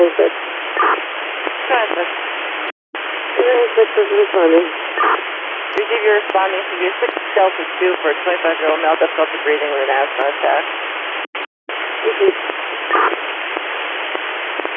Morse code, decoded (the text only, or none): none